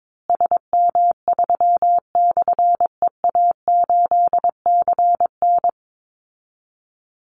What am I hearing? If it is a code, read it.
Morse code, heard as a keyed tone, about 22 wpm, 690 Hz. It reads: SM3/EA8CN
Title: Call Signs Repeated 2x - 22wpm